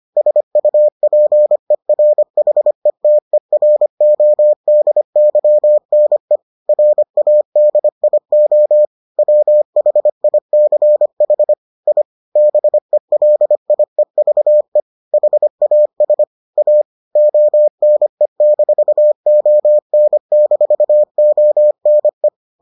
Morse code, heard as a keyed tone, about 25 wpm, 600 Hz. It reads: SUPERHETERODYNE RADIO WHICH I BELIEVE HAS A ONE-ON-ONE